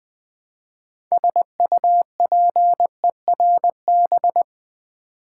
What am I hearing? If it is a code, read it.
Morse code, heard as a keyed tone, about 20 wpm, 700 Hz. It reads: SUPERB
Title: Build A Word - 3 Letters At A Time - 6-Letter Words - 20wpm